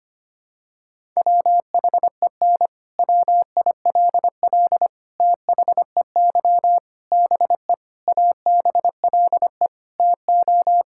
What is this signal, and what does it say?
Morse code, heard as a keyed tone, about 25 wpm, 700 Hz. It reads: WHEN WILL THEY BE ABLE TO
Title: Sentences from Top 1000 Words - Review Part 5 - 25wpm